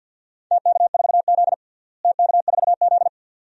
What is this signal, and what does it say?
Morse code, heard as a keyed tone, about 50 wpm, 700 Hz. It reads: TK4Z TK4Z
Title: DXCC-100 Call Signs - 4 Characters - Repeated 2x - 50wpm